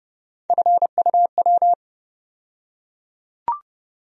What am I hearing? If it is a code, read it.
Morse code, heard as a keyed tone, about 32 wpm, 700 Hz. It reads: FUW
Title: USA Call Sign Postfixes - 3 Letters 30wpm